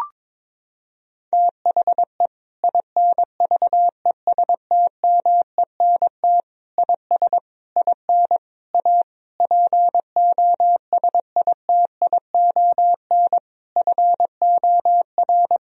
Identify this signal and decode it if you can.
Morse code, heard as a keyed tone, about 22 wpm, 700 Hz. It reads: THE INVESTMENT IS IN A POSITION FOR
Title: Sentences from Top 1000 Words - Review Part 4 - 22wpm